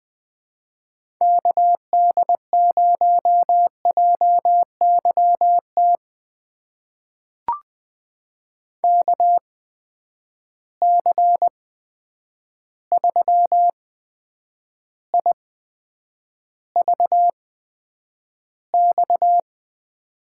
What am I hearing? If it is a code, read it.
Morse code, heard as a keyed tone, about 20 wpm, spaced as for 9 wpm, 700 Hz. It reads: KD0JYT KC3IVX
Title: USA Call Signs - 6 Characters - 8x Character Spacing - 20wpm